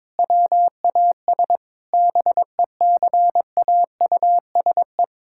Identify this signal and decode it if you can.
Morse code, heard as a keyed tone, about 22 wpm, 700 Hz. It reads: WAS BECAUSE
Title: Sentences from Top 1000 Words - Review Part 6 - 22wpm